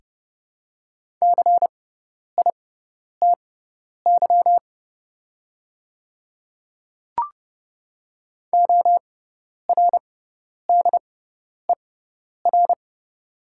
Morse code, heard as a keyed tone, about 30 wpm, spaced as for 10 wpm, 700 Hz. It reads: CITY ORDER
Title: 6x Character Spacing - Top 500 Words - 30wpm